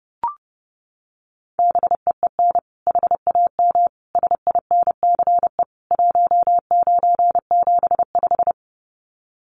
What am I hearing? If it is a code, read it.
Morse code, heard as a keyed tone, about 30 wpm, 700 Hz. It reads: BEEN HAM SINCE 1975
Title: Common QSO Phrases 30wpm